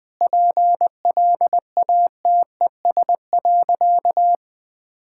Morse code, heard as a keyed tone, about 21 wpm, 700 Hz. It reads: PLATES.
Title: Easy Sentences with Hints - 20wpm